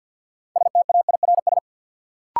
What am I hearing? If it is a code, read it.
Morse code, heard as a keyed tone, about 52 wpm, 700 Hz. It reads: STAIRS
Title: Sets of 5 Words Encore - New Word Drill - 50wpm